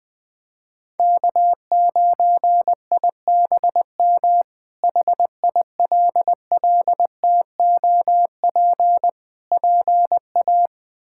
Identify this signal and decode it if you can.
Morse code, heard as a keyed tone, about 20 wpm, 700 Hz. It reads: K9IBM HILLTOP PA
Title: K1USN SST QSO Practice - 20wpm